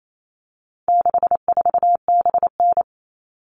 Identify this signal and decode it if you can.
Morse code, heard as a keyed tone, about 28 wpm, 700 Hz. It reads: 64BN
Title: All Cut Numbers - 4-Digit 28wpm